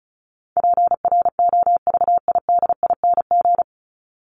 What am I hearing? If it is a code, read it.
Morse code, heard as a keyed tone, about 34 wpm, 700 Hz. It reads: PROVIDING
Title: ING Suffix - Top 100 Words with Base Words 35wpm